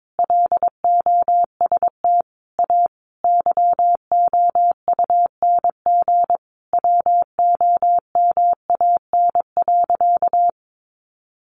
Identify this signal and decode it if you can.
Morse code, heard as a keyed tone, about 22 wpm, 700 Hz. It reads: LOST A YOUNG WOMAN.
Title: Sentences from Top 1400 Words 22wpm